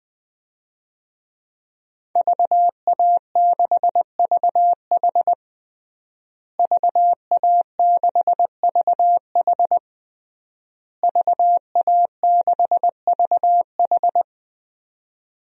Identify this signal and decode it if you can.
Morse code, heard as a keyed tone, about 20 wpm, 700 Hz. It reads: VA6VH VA6VH VA6VH
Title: DXCC-100 Call Signs - 5 Characters - Repeated 3x - 20wpm